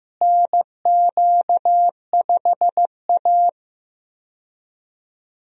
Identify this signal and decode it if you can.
Morse code, heard as a keyed tone, about 16 wpm, 700 Hz. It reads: NQ5A
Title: USA Call Signs - 4 Characters - Repeated 2x - 15wpm